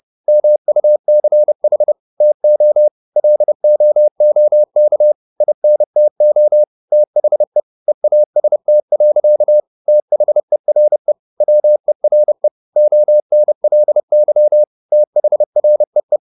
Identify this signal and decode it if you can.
Morse code, heard as a keyed tone, about 30 wpm, 600 Hz. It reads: MUCH TO LOOK INTO THE EAST. THERE WERE ONLY THREE
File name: the old man and the sea / the old man and the sea 1 30wpm